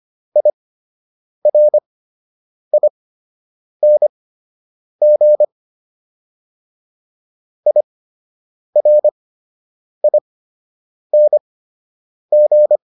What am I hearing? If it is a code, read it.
Morse code, heard as a keyed tone, about 25 wpm, spaced as for 8 wpm, 600 Hz. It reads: IRING IRING